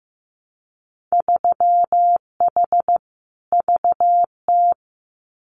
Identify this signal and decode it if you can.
Morse code, heard as a keyed tone, about 15 wpm, 700 Hz. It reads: 3H VT
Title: Winter Field Day Exchanges Encore 15wpm